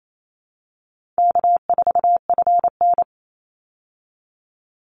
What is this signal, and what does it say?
Morse code, heard as a keyed tone, about 28 wpm, 700 Hz. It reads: K4FN
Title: Easy Call Signs Repeated 2x - 28wpm